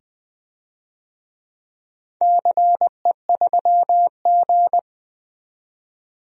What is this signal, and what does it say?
Morse code, heard as a keyed tone, about 20 wpm, 700 Hz. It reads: CE3G
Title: DXCC-100 Call Signs - 4 Characters - 20wpm